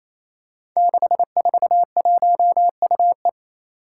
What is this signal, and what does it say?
Morse code, heard as a keyed tone, about 27 wpm, 700 Hz. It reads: 641UE
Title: Five Challenging Letter-Number Combinations 28wpm